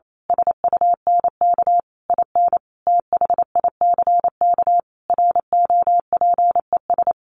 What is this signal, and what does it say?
Morse code, heard as a keyed tone, about 28 wpm, 700 Hz. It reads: SUNK IN THICK ROPES
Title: A Sound Of Thunder 28wpm